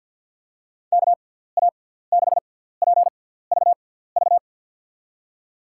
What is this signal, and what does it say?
Morse code, heard as a keyed tone, about 52 wpm, spaced as for 17 wpm, 700 Hz. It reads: KA6PVV
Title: USA Call Signs - 6 Characters - 6x Character Spacing - 50wpm